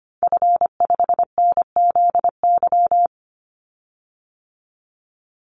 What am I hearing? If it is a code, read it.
Morse code, heard as a keyed tone, about 25 wpm, 700 Hz. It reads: F5NZY
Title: Call Signs Repeated 3x - 25wpm